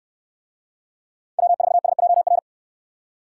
Morse code, heard as a keyed tone, about 69 wpm, 700 Hz. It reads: W4SJR